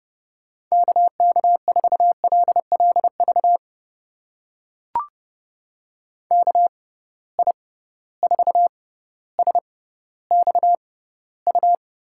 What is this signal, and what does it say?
Morse code, heard as a keyed tone, about 30 wpm, spaced as for 15 wpm, 700 Hz. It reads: KK4LLV KI4SXU
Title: USA Call Signs - 6 Characters - 6x Character Spacing - 30wpm